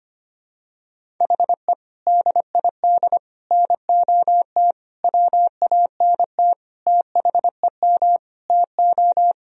Morse code, heard as a keyed tone, about 25 wpm, 700 Hz. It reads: HE DID NOT WANT THEM TO
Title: Sentences from Top 1000 Words - Review Part 1 - 25wpm